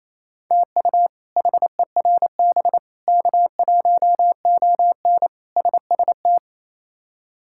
Morse code, heard as a keyed tone, about 28 wpm, 700 Hz. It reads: TU HERB K1ON SST
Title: K1USN SST QSO Practice - 28wpm